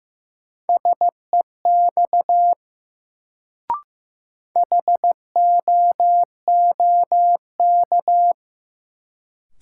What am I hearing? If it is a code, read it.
Morse code, heard as a keyed tone, about 15 wpm, 700 Hz. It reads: SEX HOOK
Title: Sentences from Top 1600 Words - New Word Drill - 15wpm